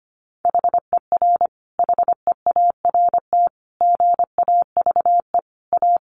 Morse code, heard as a keyed tone, about 25 wpm, 700 Hz. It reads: HER HEART GAVE A
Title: Sentences from Top 1300 Words 25wpm